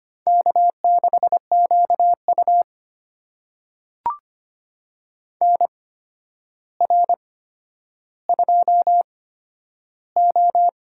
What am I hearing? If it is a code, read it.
Morse code, heard as a keyed tone, about 25 wpm, spaced as for 11 wpm, 700 Hz. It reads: K6QU NR2O
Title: USA Call Signs - 4 Characters - 8x Character Spacing - 25wpm